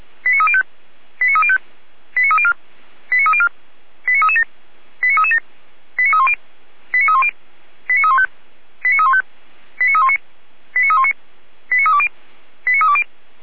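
Five-tone selective call (ZVEI), heard as quick runs of five tones, tones of about 70 ms. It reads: 89305 89305 89304 89304 89337 89337 89310 89310 89315 89315 89319 89319 89320 89320
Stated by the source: Five-tone ZVEI selective call received by SDR, from feuerwehr-orlamuende.org.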